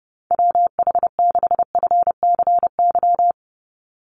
Morse code, heard as a keyed tone, about 30 wpm, 700 Hz. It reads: WH6FCY
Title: USA Call Signs 30wpm